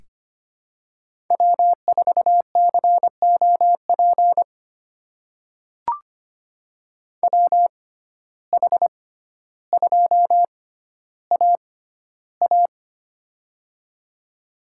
Morse code, heard as a keyed tone, about 25 wpm, spaced as for 13 wpm, 700 Hz. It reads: W4COP WH2AA
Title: USA Call Signs - 6x Character Spacing - 25wpm